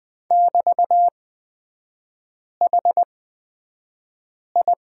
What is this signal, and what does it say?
Morse code, heard as a keyed tone, about 20 wpm, spaced as for 5 wpm, 700 Hz. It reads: <BT>HI
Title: 224 Lesson 36e - BT - Three Characters 20wpm